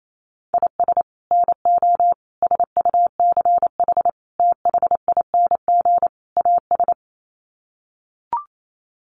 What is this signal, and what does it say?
Morse code, heard as a keyed tone, about 28 wpm, 700 Hz. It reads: IS NO SUCH THING AS
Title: Sets of 5 Words Encore 28wpm